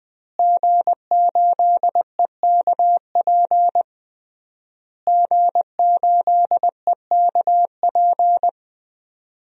Morse code, heard as a keyed tone, about 21 wpm, 700 Hz. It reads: G8EKP G8EKP
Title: DXCC-100 Call Signs - 5 Characters - Repeated 3x - 20wpm